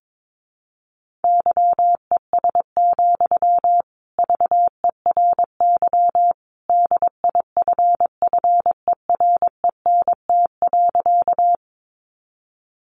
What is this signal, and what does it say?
Morse code, heard as a keyed tone, about 22 wpm, 700 Hz. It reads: YES, VERY DIFFERENT.
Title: Sentences from Top 600 Words 22wpm